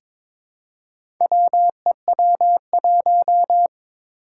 Morse code, heard as a keyed tone, about 22 wpm, 700 Hz. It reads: WEW1
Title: Four Challenging Letter-Number Combinations 22wpm